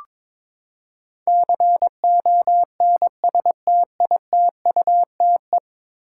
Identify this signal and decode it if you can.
Morse code, heard as a keyed tone, about 21 wpm, 700 Hz. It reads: CONSTITUTE
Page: morse-code-ninja-practice-22wpm